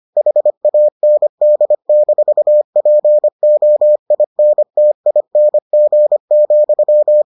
Morse code, heard as a keyed tone, about 25 wpm, 600 Hz. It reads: HAND-POINTING,